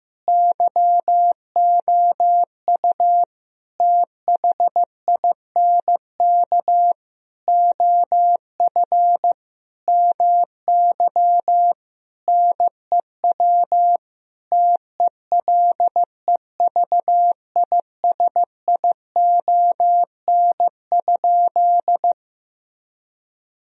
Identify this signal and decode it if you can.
Morse code, heard as a keyed tone, about 15 wpm, 700 Hz. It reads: YOU THINK OF MY NEW TELEVISION?
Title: Sentences from Top 1000 Words - Review Part 4 - 15wpm